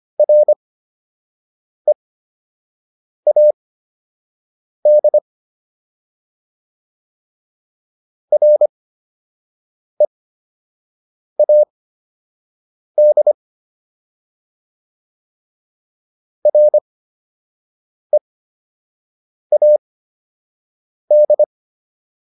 Morse code, heard as a keyed tone, about 25 wpm, spaced as for 6 wpm, 600 Hz. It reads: READ READ READ